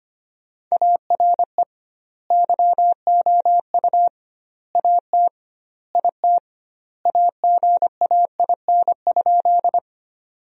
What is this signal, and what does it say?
Morse code, heard as a keyed tone, about 25 wpm, 700 Hz. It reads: ARE YOU AT IT AGAIN?
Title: 2x Word-Spacing - Sentences from Top 200 Words - 25wpm